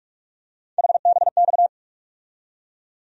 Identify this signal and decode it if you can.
Morse code, heard as a keyed tone, about 46 wpm, 700 Hz. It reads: HBX
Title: USA Call Sign Postfixes - 3 Letters 45wpm